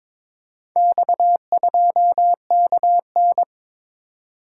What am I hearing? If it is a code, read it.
Morse code, heard as a keyed tone, about 22 wpm, 700 Hz. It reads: X2KN
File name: practice / 22wpm / Four Letter-Number Combinations - 22wpm